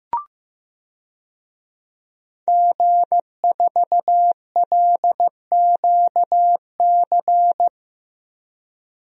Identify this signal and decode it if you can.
Morse code, heard as a keyed tone, about 15 wpm, 700 Hz. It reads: G4LQC
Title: DXCC-100 Call Signs - 5 Characters - Repeated 2x - 15wpm